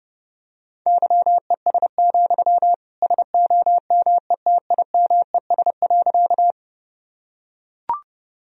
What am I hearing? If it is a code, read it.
Morse code, heard as a keyed tone, about 30 wpm, 700 Hz. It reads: YES, SOMETIMES.
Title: Sentences from Top 1000 Words - Review Part 3 - 30wpm